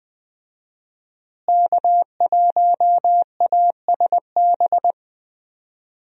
Morse code, heard as a keyed tone, about 20 wpm, 700 Hz. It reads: K1ASB
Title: USA Call Signs - 5 Characters - Repeated 3x - 20wpm